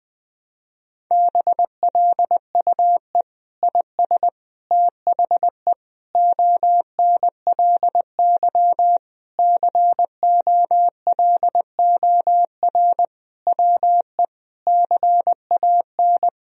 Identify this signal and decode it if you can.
Morse code, heard as a keyed tone, about 20 wpm, 700 Hz. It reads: BLUE IS THE ONLY COLOR WE CAN
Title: Sentences from Top 1000 Words - Review Part 3 - 20wpm